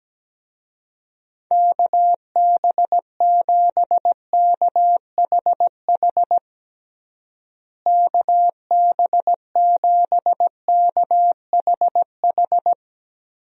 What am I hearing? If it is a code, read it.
Morse code, heard as a keyed tone, about 17 wpm, 700 Hz. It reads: KB7KHH KB7KHH
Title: USA Call Signs - 6 Characters - Repeated 2x - 17wpm